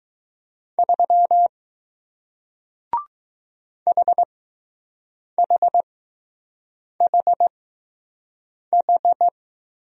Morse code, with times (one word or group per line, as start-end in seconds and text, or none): 0.8-1.5 3
3.9-9.3 HHHH